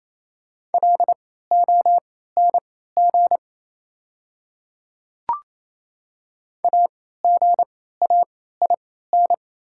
Morse code, heard as a keyed tone, about 28 wpm, spaced as for 16 wpm, 700 Hz. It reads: LONG AGAIN
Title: 3x Character Spacing - Top 200 Words - 28wpm